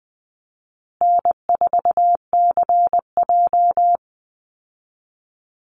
Morse code, heard as a keyed tone, about 20 wpm, 700 Hz. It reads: N4CJ
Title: Easy Call Signs Repeated 3x - 20wpm